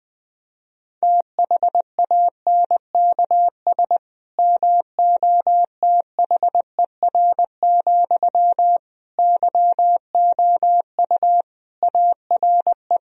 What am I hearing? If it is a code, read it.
Morse code, heard as a keyed tone, about 21 wpm, 700 Hz. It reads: THANKS MOTHER, YOU ARE
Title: Sentences from Top 1000 Words - Review Part 3 - 20wpm